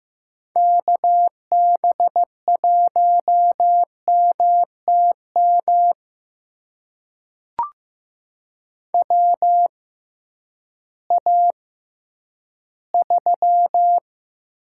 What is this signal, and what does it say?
Morse code, heard as a keyed tone, about 15 wpm, spaced as for 10 wpm, 700 Hz. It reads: KB1MTM WA3
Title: USA Call Signs - 6x Character Spacing - 15wpm